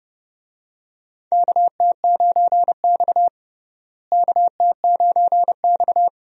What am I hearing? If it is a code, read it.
Morse code, heard as a keyed tone, about 30 wpm, 700 Hz. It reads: KT9X KT9X
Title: USA Call Signs - 4 Characters - Repeated 2x - 30wpm